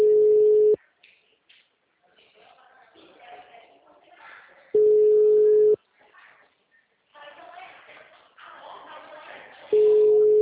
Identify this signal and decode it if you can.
Ringback tone (Europe, 425 Hz), with speech.